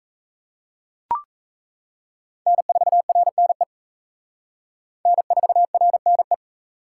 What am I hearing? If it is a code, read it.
Morse code, heard as a keyed tone, about 34 wpm, 700 Hz. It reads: N4RNE N4RNE
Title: Speed Racing - USA 1x3 Call Signs 28wpm